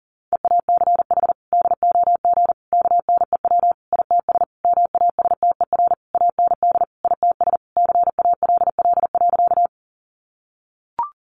Morse code, heard as a keyed tone, about 40 wpm, 700 Hz. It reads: EACH DOG KNEW ITS MASTER AND ITS CALL.
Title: Sentences from Top 1300 Words 40wpm